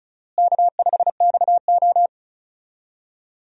Morse code, heard as a keyed tone, about 36 wpm, 700 Hz. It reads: K5XO
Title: USA Call Signs - 4 Characters - Repeated 2x - 35wpm